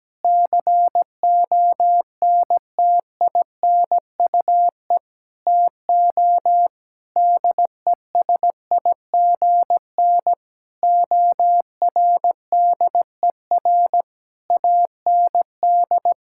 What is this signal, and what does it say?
Morse code, heard as a keyed tone, about 17 wpm, 700 Hz. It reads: CONTINUE TO DESIGN ORDER AND
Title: Sentences from Top 1000 Words - Review Part 6 - 17wpm